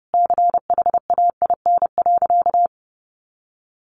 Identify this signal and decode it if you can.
Morse code, heard as a keyed tone, about 30 wpm, 700 Hz. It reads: CHAIN.
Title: Sentences from Top 1500 Words 30wpm